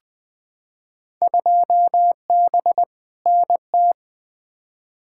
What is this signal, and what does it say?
Morse code, heard as a keyed tone, about 20 wpm, 700 Hz. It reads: 2B NT
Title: Field Day Exchanges Encore - 20wpm